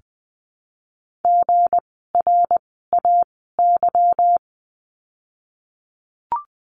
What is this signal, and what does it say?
Morse code, heard as a keyed tone, about 20 wpm, 700 Hz. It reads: G R A Y
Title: 160 Lesson 26f - Letter g - Words 20wpm